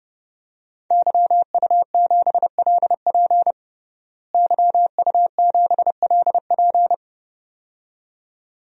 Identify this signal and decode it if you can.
Morse code, heard as a keyed tone, about 30 wpm, 700 Hz. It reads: YU7LP YU7LP
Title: DXCC-100 Call Signs - 5 Characters - Repeated 2x - 30wpm